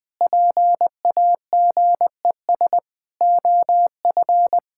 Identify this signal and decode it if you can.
Morse code, heard as a keyed tone, about 20 wpm, 700 Hz. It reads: PAGES OF
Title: Pangram Challenge v2 - 20wpm